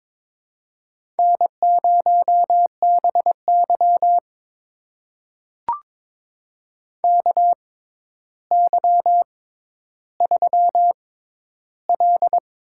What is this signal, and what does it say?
Morse code, heard as a keyed tone, about 23 wpm, spaced as for 11 wpm, 700 Hz. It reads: N0BY KY3L
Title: USA Call Signs - 4 Characters - 6x Character Spacing - 22wpm